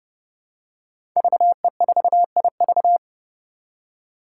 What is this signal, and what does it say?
Morse code, heard as a keyed tone, about 30 wpm, 700 Hz. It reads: VE4IV
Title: Five Challenging Letter-Number Combinations 30wpm